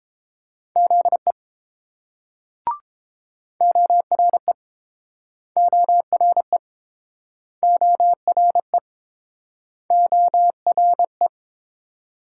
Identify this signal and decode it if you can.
Morse code, heard as a keyed tone, about 27 wpm, 700 Hz. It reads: ZE ORE ORE ORE ORE
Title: Speed-Racing Multi-letter Phonemes - 22wpm